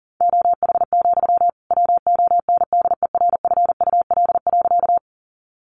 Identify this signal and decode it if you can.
Morse code, heard as a keyed tone, about 40 wpm, 700 Hz. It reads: OH, WONDERFUL.
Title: Sentences from Top 1100 Words 40wpm